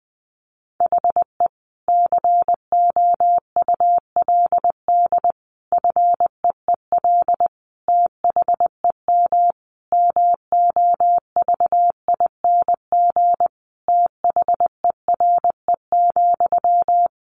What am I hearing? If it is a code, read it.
Morse code, heard as a keyed tone, about 20 wpm, 700 Hz. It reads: HE COULD FEEL THEM MOVING THERE,
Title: A Sound Of Thunder 20wpm